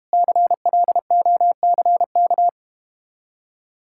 Morse code, heard as a keyed tone, about 31 wpm, 700 Hz. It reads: CLOCK
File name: Easy Words - Speed-Racing with Hints - 25wpm